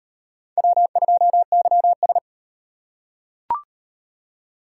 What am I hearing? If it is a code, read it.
Morse code, heard as a keyed tone, about 36 wpm, 700 Hz. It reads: W2YS